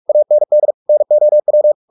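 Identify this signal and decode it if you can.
Morse code, heard as a keyed tone, about 44 wpm, 600 Hz. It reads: AND NOW
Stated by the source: Morse practice at 45 wpm, the novel Never by Ken Follett in English, from CW med Gnister.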